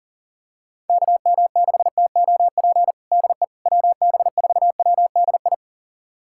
Morse code, heard as a keyed tone, about 40 wpm, 700 Hz. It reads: KM6TOP DE WB4WDI
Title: USA Callsign DE Callsign - 6 Characters - 40wpm